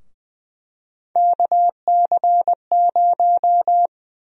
Morse code, heard as a keyed tone, about 20 wpm, 700 Hz. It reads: KC0
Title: USA Call Sign Prefixes - Repeated 2x - 20wpm